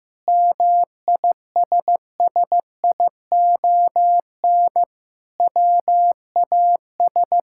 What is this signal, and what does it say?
Morse code, heard as a keyed tone, about 15 wpm, 700 Hz. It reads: MISSION WAS
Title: Sentences from Top 1600 Words 15wpm